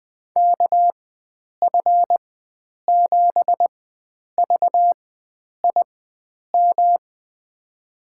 Morse code, heard as a keyed tone, about 21 wpm, spaced as for 9 wpm, 700 Hz. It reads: KF7VIM
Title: USA Call Signs - 6 Characters - 4x Character Spacing - 20wpm